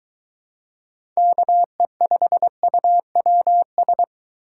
Morse code, heard as a keyed tone, about 21 wpm, 700 Hz. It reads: KE5UWS